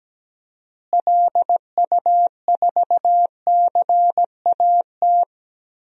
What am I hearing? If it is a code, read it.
Morse code, heard as a keyed tone, about 17 wpm, 700 Hz. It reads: LU4CAT
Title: DXCC-100 Call Signs - 6 Characters - Repeated 2x - 17wpm